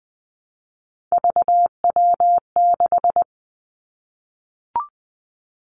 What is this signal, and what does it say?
Morse code, heard as a keyed tone, about 20 wpm, 700 Hz. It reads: VW6
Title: Three Challenging Letter-Number Combinations 20wpm